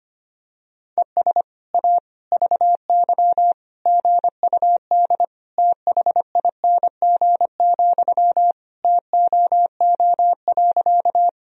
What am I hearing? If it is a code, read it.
Morse code, heard as a keyed tone, about 25 wpm, 700 Hz. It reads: ES A VY GUD THING, TOO.
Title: Sentences from Top 200 Words and Abbreviations 25wpm